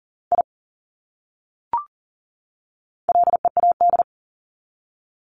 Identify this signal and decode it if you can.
Morse code, heard as a keyed tone, about 40 wpm, 700 Hz. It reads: I LEAD
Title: Top 500 Words 40wpm